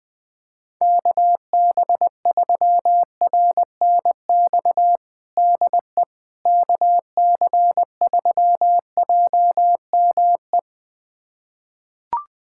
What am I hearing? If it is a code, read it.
Morse code, heard as a keyed tone, about 21 wpm, 700 Hz. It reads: KB3RNX DE KC3JME
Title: USA Callsign DE Callsign 20wpm